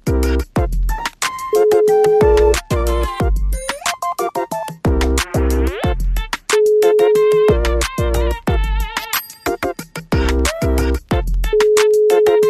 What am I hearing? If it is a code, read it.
Ringback tone (Europe, 425 Hz), with noise.